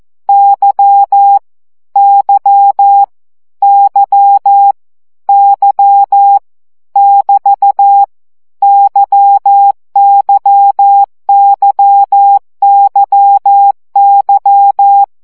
Morse code, heard as a keyed tone, about 14 wpm, 800 Hz. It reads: Y Y Y Y <BT> YYYYY
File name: CW gyakorlas y 12 baud